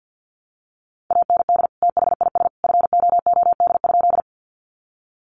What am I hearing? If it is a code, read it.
Morse code, heard as a keyed tone, about 50 wpm, 700 Hz. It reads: AND THIS FOOD?
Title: Sentences from Top 300 Words 50wpm